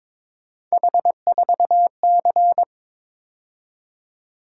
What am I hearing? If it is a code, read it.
Morse code, heard as a keyed tone, about 26 wpm, 700 Hz. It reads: H4C
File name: Speed-Racing - Three Letter-Number Combinations - 22wpm